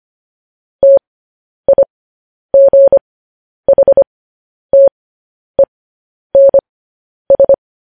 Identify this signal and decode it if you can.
Morse code, heard as a keyed tone, about 25 wpm, spaced as for 10 wpm, 550 Hz. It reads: TIGHTENS
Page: news-headlines-in-morse-code-at-10-wpm